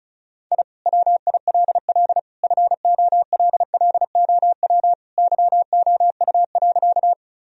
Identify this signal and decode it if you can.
Morse code, heard as a keyed tone, about 36 wpm, 700 Hz. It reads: I WILL FOLLOW YOU.
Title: Sentences from Top 1000 Words - Review Part 1 - 35wpm